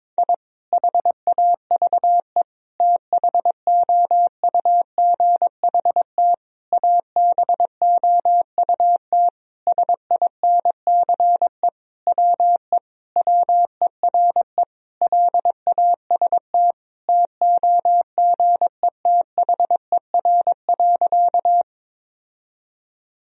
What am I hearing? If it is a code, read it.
Morse code, heard as a keyed tone, about 22 wpm, 700 Hz. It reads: I HAVE THOUGHT ABOUT SINCE WE WERE LAST TOGETHER.
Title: Sentences from Top 1000 Words - Review Part 1 - 22wpm